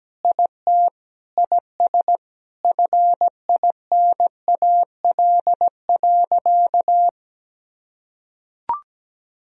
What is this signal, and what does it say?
Morse code, heard as a keyed tone, about 17 wpm, 700 Hz. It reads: IT IS FINAL.
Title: Sentences from Top 1000 Words - Review Part 2 - 17wpm